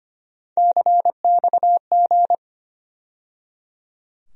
Morse code, heard as a keyed tone, about 25 wpm, 700 Hz. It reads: CXG
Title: USA Call Sign Postfixes - 3 Letters 25wpm